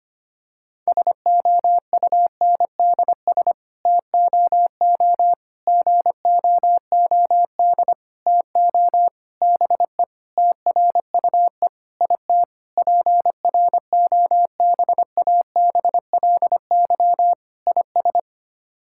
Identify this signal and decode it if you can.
Morse code, heard as a keyed tone, about 25 wpm, 700 Hz. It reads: SOUNDS TOO GOOD TO BE TRUE IT PROBABLY IS
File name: Sets of 9 Words - 25wpm